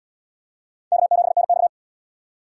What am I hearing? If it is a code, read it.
Morse code, heard as a keyed tone, about 67 wpm, 700 Hz. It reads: K7NX